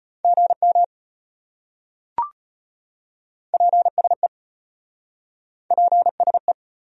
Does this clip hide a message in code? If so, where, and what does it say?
Yes: Morse code from 0.2 to 6.5 s, heard as a keyed tone, about 34 wpm, 700 Hz. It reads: GM PSE PSE